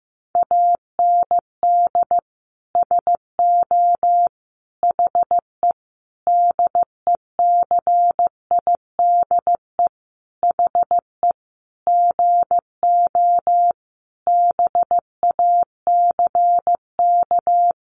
Morse code, heard as a keyed tone, about 15 wpm, 700 Hz. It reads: AND SO HE DECIDE HE GO BACK
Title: Sentences from Top 500 Words 15wpm